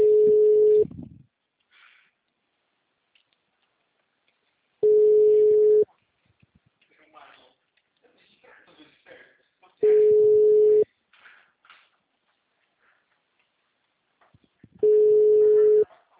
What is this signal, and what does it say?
Ringback tone (Europe, 425 Hz), with speech.